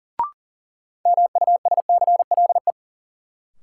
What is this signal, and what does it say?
Morse code, heard as a keyed tone, about 40 wpm, 700 Hz. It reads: MUSCLE
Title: Sentences from Top 1300 Words - New Word Drill - 40wpm